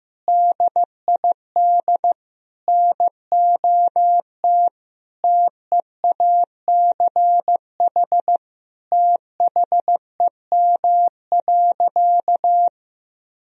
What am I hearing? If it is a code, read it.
Morse code, heard as a keyed tone, about 15 wpm, 700 Hz. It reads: DID NOT TEACH THEM.
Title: Sentences from Top 1000 Words - Review Part 2 - 15wpm